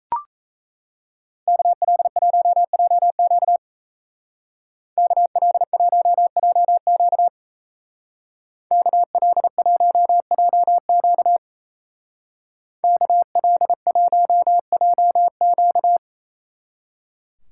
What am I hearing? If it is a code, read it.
Morse code, heard as a keyed tone, about 35 wpm, 700 Hz. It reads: KL1JQ KL1JQ KL1JQ KL1JQ